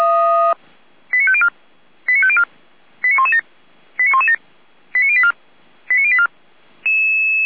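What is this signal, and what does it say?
Five-tone selective call (ZVEI), heard as quick runs of five tones, tones of about 70 ms. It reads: 89504 89504 89117 89117 89084 89084
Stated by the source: ZVEI five-tone selective call 89115, received by SDR.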